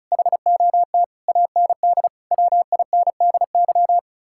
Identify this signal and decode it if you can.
Morse code, heard as a keyed tone, about 45 wpm, 700 Hz. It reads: HOT AND WINDY